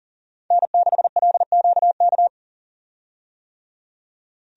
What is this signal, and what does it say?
Morse code, heard as a keyed tone, about 40 wpm, 700 Hz. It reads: N6LQK
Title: USA Call Signs - 5 Characters - 40wpm